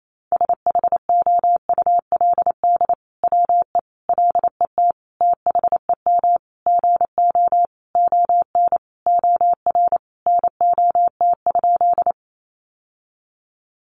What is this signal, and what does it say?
Morse code, heard as a keyed tone, about 28 wpm, 700 Hz. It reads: SHOULD WE LET THEM GO ON OR NOT?
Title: Sentences from Top 600 Words 28wpm